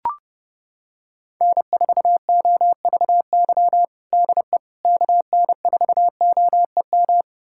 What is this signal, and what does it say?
Morse code, heard as a keyed tone, about 32 wpm, 700 Hz. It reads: N4OVY DE KN4OEM
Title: USA Callsign DE Callsign 30wpm